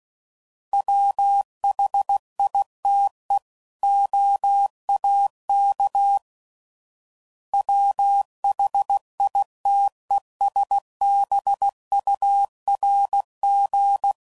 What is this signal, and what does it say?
Morse code, heard as a keyed tone, about 16 wpm, 800 Hz. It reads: WHITE OAK WHITESBURG
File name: CW Code Groups @15WPM / fromSM's / Cities